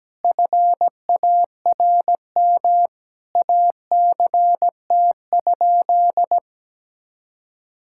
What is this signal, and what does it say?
Morse code, heard as a keyed tone, about 17 wpm, 700 Hz. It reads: FARM ACT?
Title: Sentences from Top 1000 Words - Review Part 5 - 17wpm